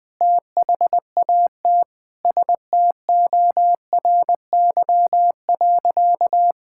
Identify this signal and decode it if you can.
Morse code, heard as a keyed tone, about 21 wpm, 700 Hz. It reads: THAT STORY.
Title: Sentences from Top 1000 Words - Review Part 5 - 20wpm